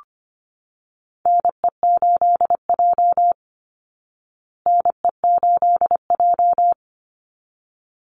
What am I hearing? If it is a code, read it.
Morse code, heard as a keyed tone, about 25 wpm, 700 Hz. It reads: NE8J NE8J
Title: Call Signs Repeated 3x - 25wpm